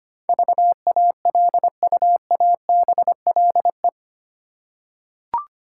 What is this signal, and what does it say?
Morse code, heard as a keyed tone, about 25 wpm, 700 Hz. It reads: VALUABLE
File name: Eight Letter Words - 25wpm